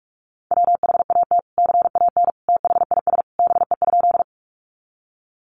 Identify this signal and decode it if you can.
Morse code, heard as a keyed tone, about 44 wpm, 700 Hz. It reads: WHAT CAN THIS BE?
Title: Sentences from Top 100 Words 45wpm